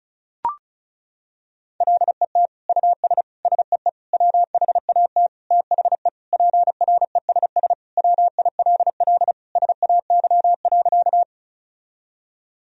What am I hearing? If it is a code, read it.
Morse code, heard as a keyed tone, about 36 wpm, 700 Hz. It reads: LET US SEE WHAT THE PRESS WILL SAY.
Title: Sentences from Top 1000 Words - Review Part 1 - 35wpm